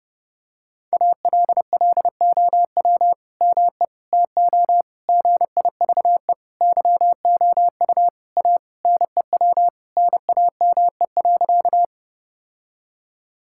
Morse code, heard as a keyed tone, about 32 wpm, 700 Hz. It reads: ALLOW ME TO GIVE YOU A NEW NAME.
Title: Sentences from Top 1000 Words - Review Part 5 - 30wpm